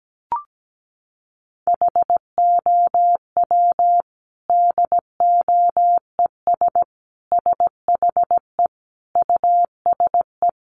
Morse code, heard as a keyed tone, about 17 wpm, 700 Hz. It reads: HOW DOES SHE USE
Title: Sentences from Top 200 Words 17wpm